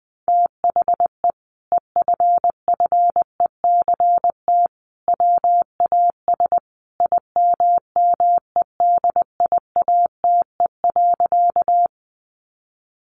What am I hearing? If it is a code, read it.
Morse code, heard as a keyed tone, about 19 wpm, 700 Hz. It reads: THE EFFECT WAS IMMEDIATE.
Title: Sentences from Top 1200 Words 20wpm